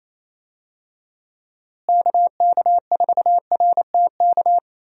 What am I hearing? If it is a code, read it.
Morse code, heard as a keyed tone, about 27 wpm, 700 Hz. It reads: KK4RTK
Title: USA Call Signs - 6 Characters - Repeated 2x - 28wpm